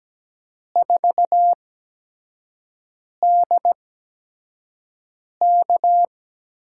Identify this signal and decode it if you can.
Morse code, heard as a keyed tone, about 17 wpm, spaced as for 5 wpm, 700 Hz. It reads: 4DK
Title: USA Call Signs - 5 Characters - 8x Character Spacing - 17wpm